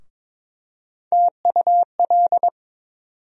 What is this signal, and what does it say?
Morse code, heard as a keyed tone, about 21 wpm, 700 Hz. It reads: TUL
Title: USA Call Sign Postfixes - 3 Letters 22wpm